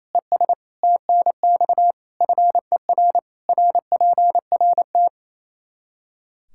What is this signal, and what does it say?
Morse code, heard as a keyed tone, about 28 wpm, 700 Hz. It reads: ES TNX FER RPRT
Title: Practice QSO - Exchange 2 - Info - 28wpm